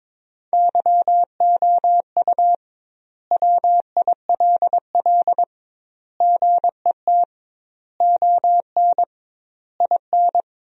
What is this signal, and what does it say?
Morse code, heard as a keyed tone, about 22 wpm, 700 Hz. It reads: YOU WILL GET ON IN
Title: 2x Word-Spacing - Sentences from Top 200 Words - 22wpm